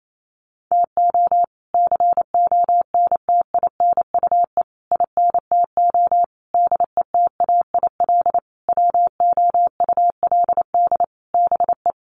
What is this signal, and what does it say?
Morse code, heard as a keyed tone, about 28 wpm, 700 Hz. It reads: TO CONTINUE INTO DETAIL WOULD BE
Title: Sentences from Top 1500 Words 28wpm